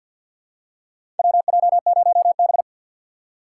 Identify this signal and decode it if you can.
Morse code, heard as a keyed tone, about 52 wpm, 700 Hz. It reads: WJ0B
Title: USA Call Signs - 4 Characters - 2x Character Spacing - 50wpm